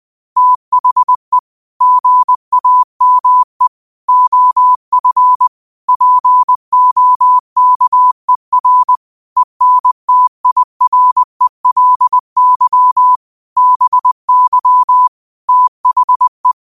Morse code, heard as a keyed tone, about 20 wpm, 1000 Hz. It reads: THE GAME OF POKER ENTIRELY BY THE
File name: qotdfiles / 20wpm